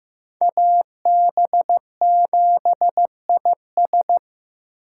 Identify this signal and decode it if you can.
Morse code, heard as a keyed tone, about 18 wpm, 700 Hz. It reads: AB7IS
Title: Speed Racing - USA Call Signs 15wpm